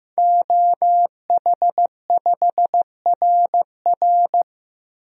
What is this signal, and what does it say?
Morse code, heard as a keyed tone, about 16 wpm, 700 Hz. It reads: OH5RR
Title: DXCC-100 Call Signs - 5 Characters - Repeated 3x - 15wpm